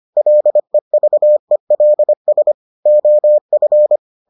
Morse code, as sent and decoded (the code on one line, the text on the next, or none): .-.. ...- . .-.. ... / --- ..-.
LEVELS OF